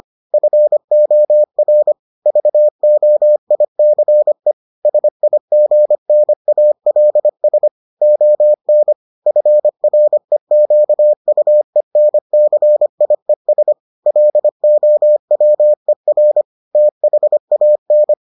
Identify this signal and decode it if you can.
Morse code, heard as a keyed tone, about 25 wpm, 600 Hz. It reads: FOR VOICE SIGNALS ON FREQUENCIES LOWER THAN